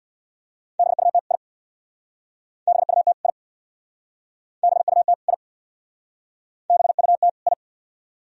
Morse code, heard as a keyed tone, about 62 wpm, 700 Hz. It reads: BUT I BUT I BUT I BUT I
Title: Speed-Racing - Sets of 2 Words 50wpm